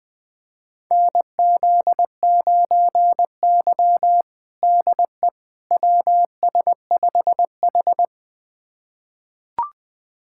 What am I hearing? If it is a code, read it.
Morse code, heard as a keyed tone, about 19 wpm, 700 Hz. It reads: NZ9Y DE WS5H
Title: USA Callsign DE Callsign - 4 Characters - 20wpm